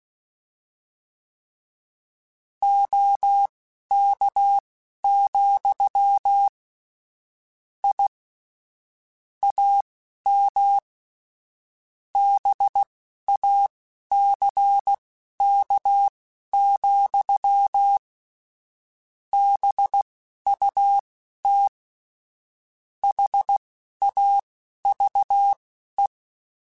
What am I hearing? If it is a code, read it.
Morse code, heard as a keyed tone, about 16 wpm, spaced as for 11 wpm, 800 Hz. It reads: OK, I AM BACK, BUT HAVE
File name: QSOs / QSOs-mp3s / kc7haa@10